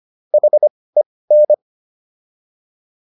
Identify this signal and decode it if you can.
Morse code, heard as a keyed tone, about 25 wpm, spaced as for 18 wpm, 600 Hz. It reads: HEN